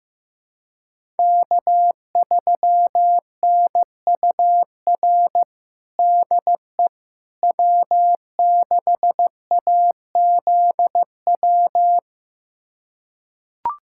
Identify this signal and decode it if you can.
Morse code, heard as a keyed tone, about 16 wpm, 700 Hz. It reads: K3NUR DE W6AZW
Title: USA Callsign DE Callsign - 5 Characters - 15wpm